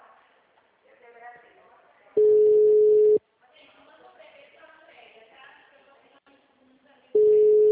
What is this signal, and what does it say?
Ringback tone (Europe, 425 Hz), with speech.
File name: out- -2001